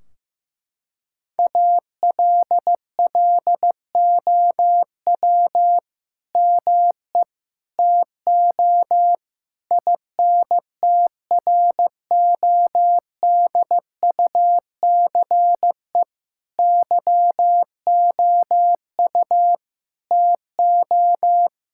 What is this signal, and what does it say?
Morse code, heard as a keyed tone, about 15 wpm, 700 Hz. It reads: ALLOW ME TO INTRODUCE YOU TO
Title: Sentences from Top 1600 Words 15wpm